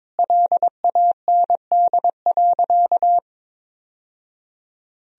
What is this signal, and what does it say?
Morse code, heard as a keyed tone, about 22 wpm, 700 Hz. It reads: LAND.
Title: Sentences from Top 1000 Words - Review Part 1 - 22wpm